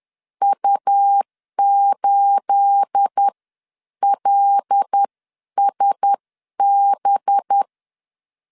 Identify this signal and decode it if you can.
Morse code, heard as a keyed tone, about 10 wpm, spaced as for 9 wpm, 800 Hz. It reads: U8LSB
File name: LEZIONI / 9 28 10wpm 2842